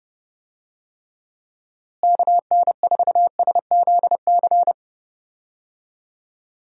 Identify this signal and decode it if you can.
Morse code, heard as a keyed tone, about 30 wpm, 700 Hz. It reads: KN4SZC
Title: USA Call Signs - 6 Characters - Repeated 2x - 30wpm